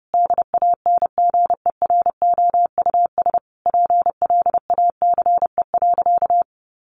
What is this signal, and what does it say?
Morse code, heard as a keyed tone, about 30 wpm, 700 Hz. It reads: DANGEROUS PLACE.
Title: Sentences from Top 1200 Words 30wpm